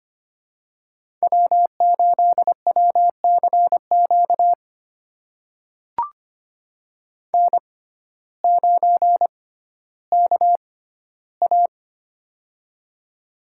Morse code, heard as a keyed tone, about 25 wpm, spaced as for 14 wpm, 700 Hz. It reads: W8WCQ N9KA
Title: USA Call Signs - 6x Character Spacing - 25wpm